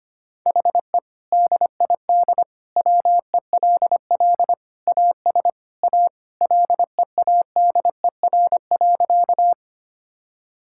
Morse code, heard as a keyed tone, about 25 wpm, 700 Hz. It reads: HE DID WELL AS A LEADER.
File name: practice / 25wpm / Sentences with 2-Word Phrases - 25wpm